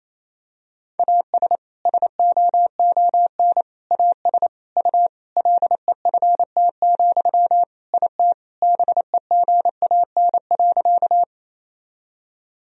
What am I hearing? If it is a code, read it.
Morse code, heard as a keyed tone, about 28 wpm, 700 Hz. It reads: AS SOON AS U LEFT, IT BEGAN.
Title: Sentences from Top 400 Words and Abbreviations 28wpm